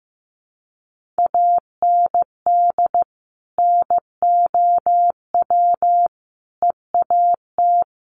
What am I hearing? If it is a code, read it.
Morse code, heard as a keyed tone, about 15 wpm, 700 Hz. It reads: AND NOW EAT
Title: Sentences from Top 1500 Words 15wpm